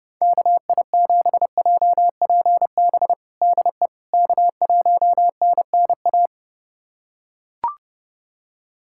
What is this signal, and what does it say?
Morse code, heard as a keyed tone, about 30 wpm, 700 Hz. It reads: KI7JPB DE K1NNA
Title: USA Callsign DE Callsign 30wpm